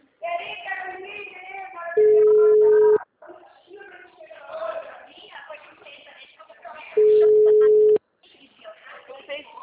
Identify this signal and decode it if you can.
Ringback tone (Europe, 425 Hz), with speech.